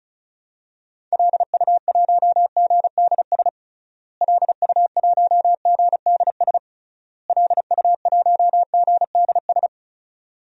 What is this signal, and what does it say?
Morse code, heard as a keyed tone, about 36 wpm, 700 Hz. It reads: LU1GDS LU1GDS LU1GDS
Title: DXCC-100 Call Signs - 6 Characters - Repeated 3x - 35wpm